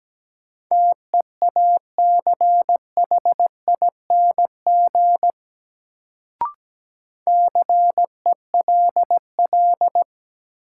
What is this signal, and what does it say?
Morse code, heard as a keyed tone, about 17 wpm, 700 Hz. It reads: TEACHING CELL
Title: Sentences from Top 1100 Words - New Word Drill Alt - 17wpm